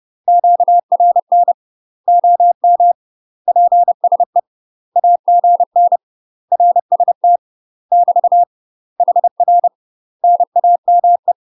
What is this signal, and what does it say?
Morse code, heard as a keyed tone, about 30 wpm, 700 Hz. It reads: QRN OM PSE AGN RST <BT> HR NAME
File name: UK MASTER QSOs TRAINING / 30WPM / UK MASTER QSOs 0051